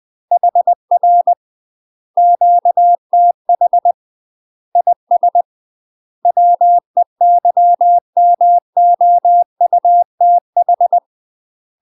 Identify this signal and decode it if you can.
Morse code, heard as a keyed tone, about 20 wpm, 700 Hz. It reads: HR QTH IS WEYMOUTH
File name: UK MASTER QSOs TRAINING / 20WPM / UK MASTER QSOs 0048